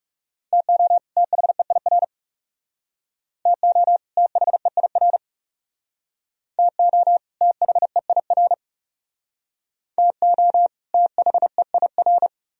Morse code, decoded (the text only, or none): TO THEIR TO THEIR TO THEIR TO THEIR